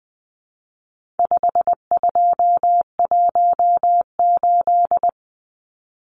Morse code, heard as a keyed tone, about 20 wpm, 700 Hz. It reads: 5218
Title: Numbers - 4-Digit 20wpm